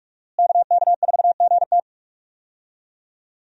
Morse code, heard as a keyed tone, about 46 wpm, 700 Hz. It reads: KK4GT
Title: USA Call Signs - 5 Characters - Repeated 2x - 45wpm